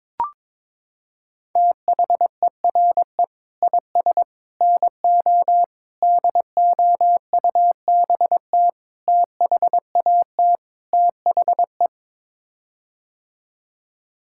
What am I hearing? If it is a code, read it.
Morse code, heard as a keyed tone, about 21 wpm, 700 Hz. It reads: THERE IS NO DOUBT THAT THE
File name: practice / 22wpm / Sets of 6 Words - 22wpm